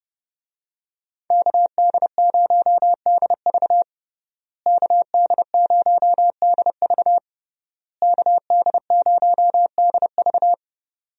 Morse code, heard as a keyed tone, about 30 wpm, 700 Hz. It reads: KD0DV KD0DV KD0DV
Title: USA Call Signs - 5 Characters - Repeated 3x - 30wpm